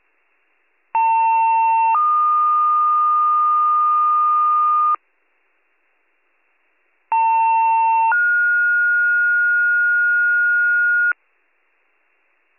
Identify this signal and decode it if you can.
Two-tone page, heard as a one-second tone, then a three-second tone, about 900 then 1220 Hz, about 900 then 1430 Hz. It is heard from 0.9 to 11.1 s.